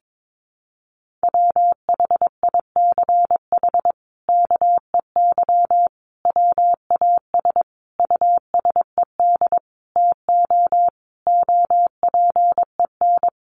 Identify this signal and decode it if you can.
Morse code, heard as a keyed tone, about 22 wpm, 700 Hz. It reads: WHICH KEY WAS USED TO OPEN
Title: Sentences from Top 600 Words 22wpm